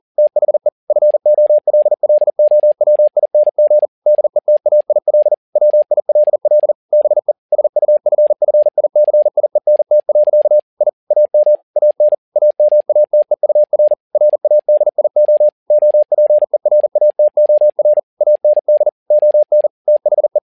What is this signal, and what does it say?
Morse code, heard as a keyed tone, about 40 wpm, 600 Hz. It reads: THE FOLLOWING DETAIL WILL BE SUFFICIENT. I AM AN AMATEUR RADIO OPERATOR AND ON THE
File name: learning center / tower trouble / tower touble 40wpm